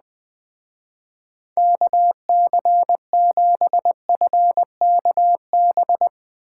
Morse code, heard as a keyed tone, about 20 wpm, 700 Hz. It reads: KC7FKB